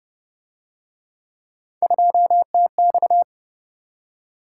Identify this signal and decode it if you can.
Morse code, heard as a keyed tone, about 32 wpm, 700 Hz. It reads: 2TX